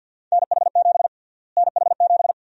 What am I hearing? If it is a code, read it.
Morse code, heard as a keyed tone, about 50 wpm, 700 Hz. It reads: NH7 NH7
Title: USA Call Sign Prefixes - Repeated 2x - 50wpm